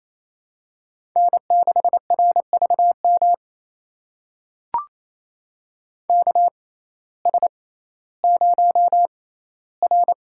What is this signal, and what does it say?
Morse code, heard as a keyed tone, about 28 wpm, spaced as for 15 wpm, 700 Hz. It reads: N6RVM KS0R
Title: USA Call Signs - 6x Character Spacing - 28wpm